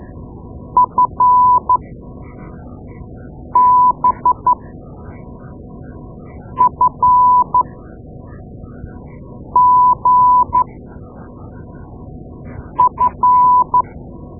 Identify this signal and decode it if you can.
Morse code, heard as a keyed tone, about 12 wpm, spaced as for 4 wpm, 1000 Hz. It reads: FBFGF
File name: Morse Lessons / Morse Lessons / Tape-2A